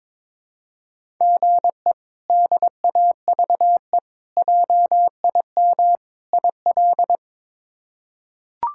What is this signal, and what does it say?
Morse code, heard as a keyed tone, about 22 wpm, 700 Hz. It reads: GE DAVE JIM IL
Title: K1USN SST QSO Practice - 22wpm